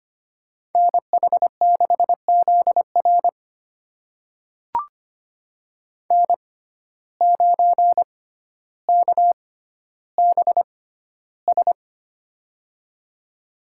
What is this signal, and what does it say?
Morse code, heard as a keyed tone, about 25 wpm, spaced as for 13 wpm, 700 Hz. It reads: NH6ZR N9KBS
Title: USA Call Signs - 5 Characters - 6x Character Spacing - 25wpm